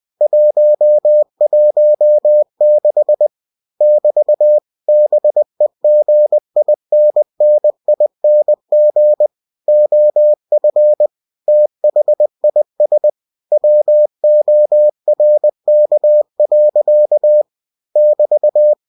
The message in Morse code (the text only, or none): 116 <BT> BEGINNING OF THIS WORK. <BT>